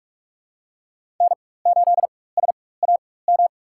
Morse code, heard as a keyed tone, about 46 wpm, spaced as for 21 wpm, 700 Hz. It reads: N8SAM
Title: USA Call Signs - 5 Characters - 4x Character Spacing - 45wpm